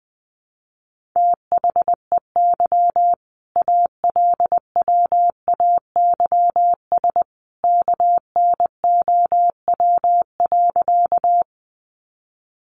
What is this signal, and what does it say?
Morse code, heard as a keyed tone, about 20 wpm, 700 Hz. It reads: THEY ALWAYS KNOW.
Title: Sentences from Top 300 Words 20wpm